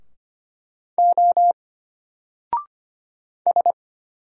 Morse code, heard as a keyed tone, about 25 wpm, 700 Hz. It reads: O S
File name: Single-Letter-Number-25wpm